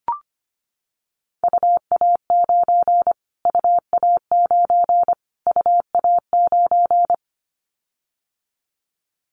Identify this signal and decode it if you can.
Morse code, heard as a keyed tone, about 25 wpm, 700 Hz. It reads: UA9 UA9 UA9
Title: Callsign Prefixes for Top 25 DXCC - Repeated 3x - 25wpm